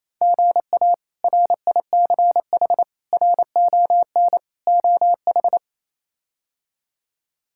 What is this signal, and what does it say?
Morse code, heard as a keyed tone, about 28 wpm, 700 Hz. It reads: GA RICH RON OH
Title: K1USN SST QSO Practice - 28wpm